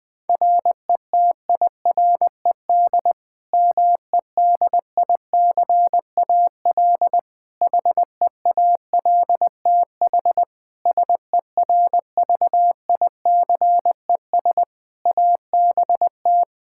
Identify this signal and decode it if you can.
Morse code, heard as a keyed tone, about 21 wpm, 700 Hz. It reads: RETIRED MEDICAL HEALTH SERVICES ABT
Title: Practice QSO - Exchange 3 - More Info - 20wpm